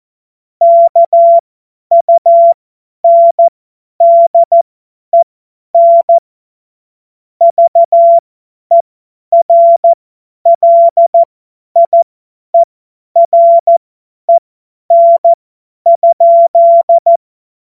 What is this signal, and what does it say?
Morse code, heard as a keyed tone, about 14 wpm, spaced as for 10 wpm, 680 Hz. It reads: KUNDEN VERLIEREN?
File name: Morsekurs / Witze / Witz 27. Januar 2026 50 F70